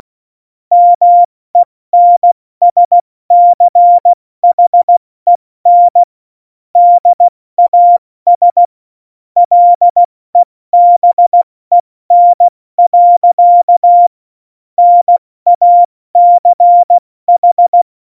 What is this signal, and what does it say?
Morse code, heard as a keyed tone, about 16 wpm, 700 Hz. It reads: MENSCHEN DAS LEBEN. NACH
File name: Morsekurs / Witze / Witz 19. Januar 2021, 70 F80